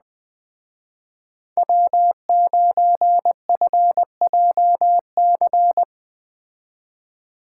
Morse code, heard as a keyed tone, about 20 wpm, 700 Hz. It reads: W9FJC
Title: Speed Racing - USA 1x3 Call Signs 17wpm